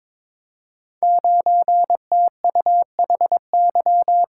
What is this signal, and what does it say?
Morse code, heard as a keyed tone, about 22 wpm, 700 Hz. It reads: 9TUHY
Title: Code Groups - 5 Letter:Numbers - 22wpm